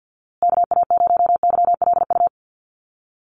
Morse code, heard as a keyed tone, about 50 wpm, 700 Hz. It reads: KA0YLU
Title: USA Call Signs Repeated 2x 50wpm